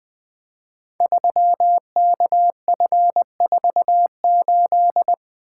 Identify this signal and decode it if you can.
Morse code, heard as a keyed tone, about 21 wpm, 700 Hz. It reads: 3KF48
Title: Code Groups - 5 Letter:Numbers - 20wpm